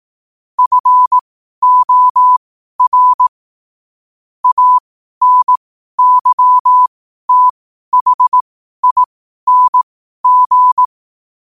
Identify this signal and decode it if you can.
Morse code, heard as a keyed tone, about 18 wpm, spaced as for 12 wpm, 1000 Hz. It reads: FOR ANYTHING
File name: qotdfiles / 12wpm